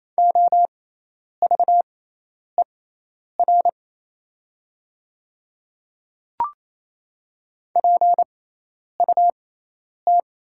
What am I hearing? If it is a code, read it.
Morse code, heard as a keyed tone, about 28 wpm, spaced as for 10 wpm, 700 Hz. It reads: OVER PUT
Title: 6x Character Spacing - Top 200 Words - 28wpm